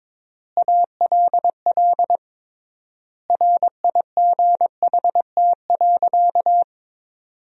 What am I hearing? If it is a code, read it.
Morse code, heard as a keyed tone, about 21 wpm, 700 Hz. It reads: ALL RIGHT.
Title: 3x Word-Spacing - Sentences from Top 200 Words - 22wpm